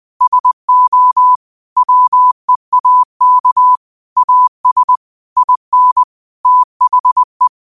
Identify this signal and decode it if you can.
Morse code, heard as a keyed tone, about 20 wpm, 1000 Hz. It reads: SO WEAK AS IN THE